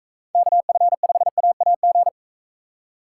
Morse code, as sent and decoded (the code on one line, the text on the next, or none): -.- ..-. ..... .- .- --.
KF5AAG